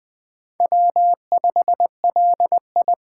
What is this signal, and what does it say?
Morse code, heard as a keyed tone, about 20 wpm, 700 Hz. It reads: W5LI
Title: USA Call Signs - 4 Characters - 6x Character Spacing - 20wpm